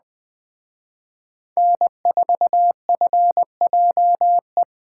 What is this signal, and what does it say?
Morse code, heard as a keyed tone, about 20 wpm, 700 Hz. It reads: N4FJE